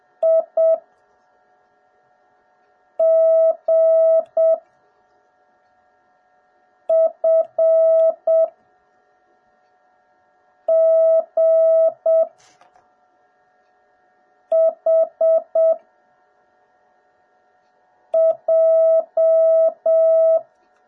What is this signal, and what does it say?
Morse code, heard as a keyed tone, about 7 wpm, 650 Hz. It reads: I G F G H J